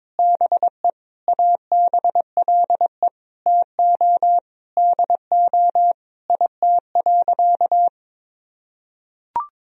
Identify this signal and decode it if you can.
Morse code, heard as a keyed tone, about 22 wpm, 700 Hz. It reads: BE ABLE TO DO IT.
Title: Sentences from Top 1000 Words - Review Part 2 - 22wpm